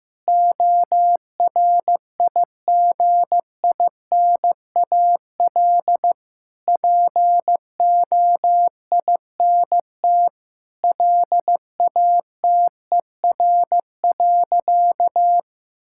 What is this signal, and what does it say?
Morse code, heard as a keyed tone, about 15 wpm, 700 Hz. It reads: ORIGINAL POINT LATER.
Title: Sentences from Top 1000 Words - Review Part 6 - 15wpm